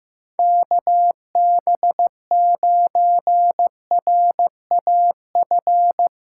Morse code, heard as a keyed tone, about 15 wpm, 700 Hz. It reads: KB9RAF
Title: USA Callsign DE Callsign 15wpm